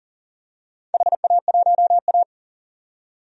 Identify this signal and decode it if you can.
Morse code, heard as a keyed tone, about 40 wpm, 700 Hz. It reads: HA1A
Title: DXCC-100 Call Signs - 4 Characters - 40wpm